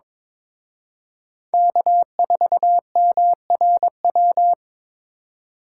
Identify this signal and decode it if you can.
Morse code, heard as a keyed tone, about 24 wpm, 700 Hz. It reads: K4MRW
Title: Speed Racing - USA Call Signs 22wpm